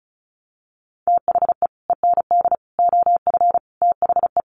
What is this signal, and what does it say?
Morse code, heard as a keyed tone, about 34 wpm, 700 Hz. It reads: THE END OF THE
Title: Sets of 5 Words Encore2 35wpm